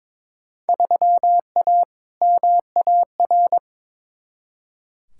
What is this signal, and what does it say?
Morse code, heard as a keyed tone, about 22 wpm, 700 Hz. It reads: 3A MAR
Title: Field Day Exchanges Encore - 22wpm